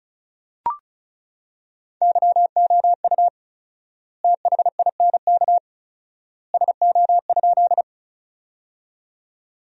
Morse code, heard as a keyed tone, about 36 wpm, 700 Hz. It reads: YOU THINK SO?
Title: 4x Word-Spacing - Sentences from Top 200 Words - 35wpm